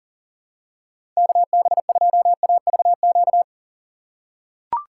Morse code, heard as a keyed tone, about 40 wpm, 700 Hz. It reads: KB2AVQ
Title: USA Call Signs - 6 Characters - 2x Character Spacing - 40wpm